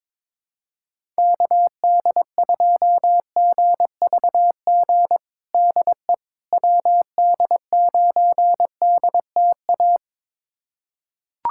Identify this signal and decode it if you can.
Morse code, heard as a keyed tone, about 22 wpm, 700 Hz. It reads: KD2GVG DE WD9DTA
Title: USA Callsign DE Callsign - 6 Characters - 22wpm